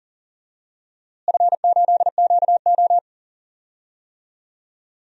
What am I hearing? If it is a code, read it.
Morse code, heard as a keyed tone, about 40 wpm, 700 Hz. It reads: F8QO
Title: DXCC-100 Call Signs - 4 Characters - Repeated 3x - 40wpm